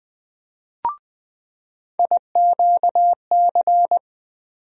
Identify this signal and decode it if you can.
Morse code, heard as a keyed tone, about 20 wpm, 700 Hz. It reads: IQC
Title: USA Call Sign Postfixes - 3 Letters 20wpm